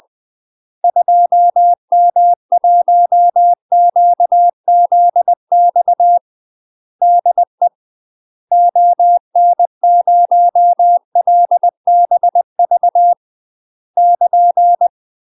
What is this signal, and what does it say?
Morse code, heard as a keyed tone, about 20 wpm, 700 Hz. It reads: 2M1QZX DE ON0LBV <KN>
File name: UK MASTER QSOs TRAINING/20WPM